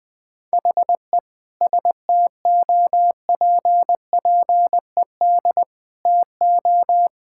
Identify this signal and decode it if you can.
Morse code, heard as a keyed tone, about 21 wpm, 700 Hz. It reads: HE STOPPED TO
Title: Sentences with 2-Word Phrases - Spoken Beginning - 20wpm